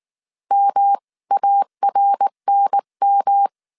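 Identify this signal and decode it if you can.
Morse code, heard as a keyed tone, about 19 wpm, 800 Hz. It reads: MARNM
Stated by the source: Morse code dictation at 18 wpm from the IW3SGT QRS Morse code training course, five-character groups of 300 characters.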